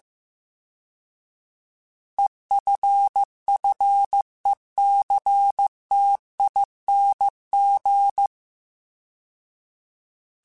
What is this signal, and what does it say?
Morse code, heard as a keyed tone, about 15 wpm, 800 Hz. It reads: EFFECTING
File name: Alphabetical / Alphabetical mp3s / Es @13